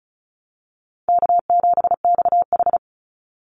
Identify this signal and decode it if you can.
Morse code, heard as a keyed tone, about 34 wpm, 700 Hz. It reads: K7XH
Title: Easy Call Signs Repeated 3x - 35wpm